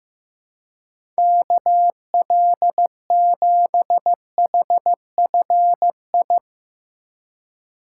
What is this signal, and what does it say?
Morse code, heard as a keyed tone, about 15 wpm, 700 Hz. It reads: KL7HFI
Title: K1USN SST QSO Practice - 15wpm